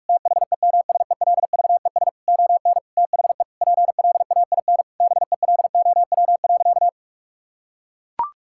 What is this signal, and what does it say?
Morse code, heard as a keyed tone, about 41 wpm, 700 Hz. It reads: THEMSELVES ON THE PLAIN BELOW.
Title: Sentences from Top 1000 Words - Review Part 5 - 45wpm